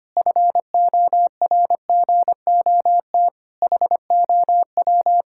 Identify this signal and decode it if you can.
Morse code, heard as a keyed tone, about 25 wpm, 700 Hz. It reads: FORGOT HOW
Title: Pangram Challenge v2 - 25wpm